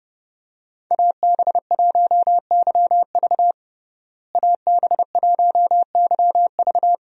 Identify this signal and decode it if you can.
Morse code, heard as a keyed tone, about 32 wpm, 700 Hz. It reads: AB1YV AB1YV
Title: USA Call Signs - 5 Characters - Repeated 2x - 30wpm